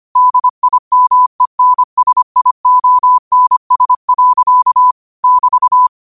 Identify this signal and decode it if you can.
Morse code, heard as a keyed tone, about 25 wpm, 1000 Hz. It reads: DIMENSIONS. <BT>